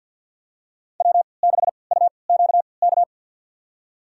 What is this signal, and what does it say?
Morse code, heard as a keyed tone, about 49 wpm, spaced as for 28 wpm, 700 Hz. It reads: W6UQK
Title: USA Call Signs - 5 Characters - 3x Character Spacing - 50wpm